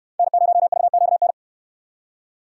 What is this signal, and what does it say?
Morse code, heard as a keyed tone, about 64 wpm, 700 Hz. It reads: N0VQN